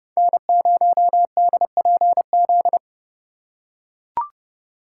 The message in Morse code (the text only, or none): N0DPZ